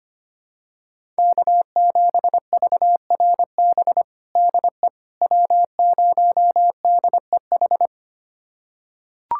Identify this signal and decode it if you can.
Morse code, heard as a keyed tone, about 25 wpm, 700 Hz. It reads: K7VRB DE W0DEH
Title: USA Callsign DE Callsign - 5 Characters - 25wpm